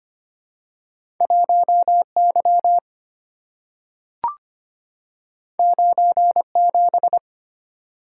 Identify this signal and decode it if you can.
Morse code, heard as a keyed tone, about 25 wpm, 700 Hz. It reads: 1 Y 9 7
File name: Two Letter-Number Combinations - 25wpm